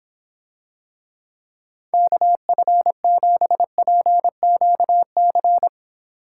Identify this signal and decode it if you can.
Morse code, heard as a keyed tone, about 26 wpm, 700 Hz. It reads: KF7PQC